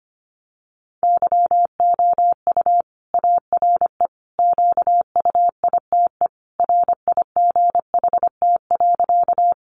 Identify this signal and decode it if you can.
Morse code, heard as a keyed tone, about 25 wpm, 700 Hz. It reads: YOU ARE QUITE RIGHT.
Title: Sentences from Top 700 Words 25wpm